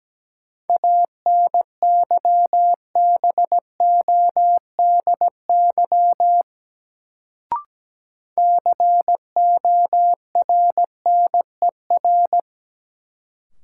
Morse code, heard as a keyed tone, about 17 wpm, 700 Hz. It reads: ANYBODY CORNER
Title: Sets of 5 Words - New Word Drill - 17wpm